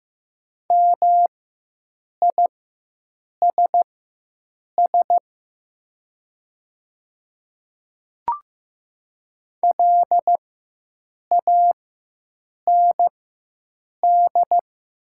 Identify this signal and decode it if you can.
Morse code, heard as a keyed tone, about 15 wpm, spaced as for 7 wpm, 700 Hz. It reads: MISS LAND
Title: 4x Character Spacing - Top 500 Words - 15wpm